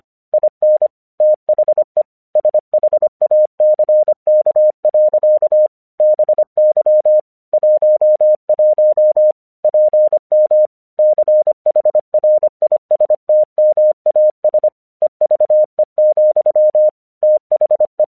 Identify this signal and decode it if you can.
Morse code, heard as a keyed tone, about 25 wpm, 610 Hz. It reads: IN THE SHACK. BY 11 PM CHRISTMAS EVE, THE
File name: 1 English 25wpm / AQC27